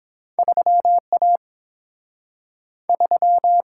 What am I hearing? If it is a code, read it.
Morse code, heard as a keyed tone, about 31 wpm, 700 Hz. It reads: 3A 3